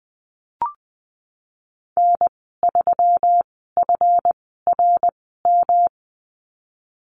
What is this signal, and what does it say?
Morse code, heard as a keyed tone, about 20 wpm, spaced as for 15 wpm, 700 Hz. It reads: N3FRM
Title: 161 Lesson 26g - Letter g - Call Signs 20wpm